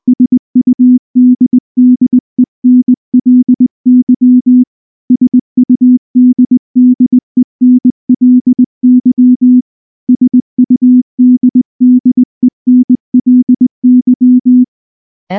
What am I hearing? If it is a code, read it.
Morse code, heard as a keyed tone, about 18 wpm, 250 Hz. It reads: SUDDENLY SUDDENLY SUDDENLY
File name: morse / morse-training-20WPM-3